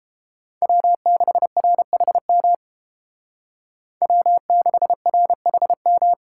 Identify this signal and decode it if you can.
Morse code, heard as a keyed tone, about 33 wpm, 700 Hz. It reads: W6RHM W6RHM